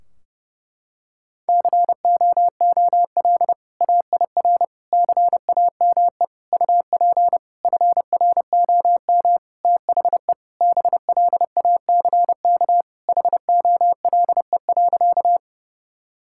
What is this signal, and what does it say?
Morse code, heard as a keyed tone, about 30 wpm, 700 Hz. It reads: COOL AIR CAME UP FROM THE BLACK HOLE.
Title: Sentences from Top 1600 Words 30wpm